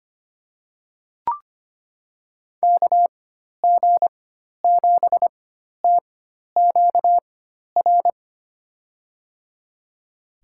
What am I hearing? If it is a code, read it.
Morse code, heard as a keyed tone, about 25 wpm, spaced as for 12 wpm, 700 Hz. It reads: KG7TQR
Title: USA Call Signs - 6 Characters - 4x Character Spacing - 25wpm